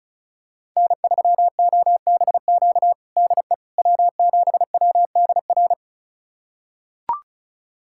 Morse code, heard as a keyed tone, about 36 wpm, 700 Hz. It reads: N3OBQ DE W7WDR
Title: USA Callsign DE Callsign - 5 Characters - 35wpm